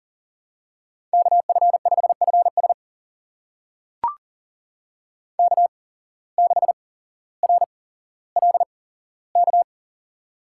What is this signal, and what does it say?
Morse code, heard as a keyed tone, about 38 wpm, spaced as for 17 wpm, 700 Hz. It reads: KF5FS K6RLK
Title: USA Call Signs - 5 Characters - 8x Character Spacing - 40wpm